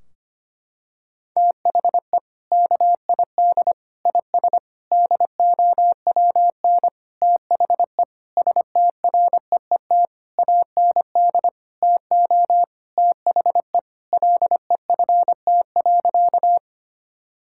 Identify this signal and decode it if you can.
Morse code, heard as a keyed tone, about 25 wpm, 700 Hz. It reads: THE KID IS DOWN THE STREET AND TO THE LEFT.
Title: Sentences from Top 1600 Words 25wpm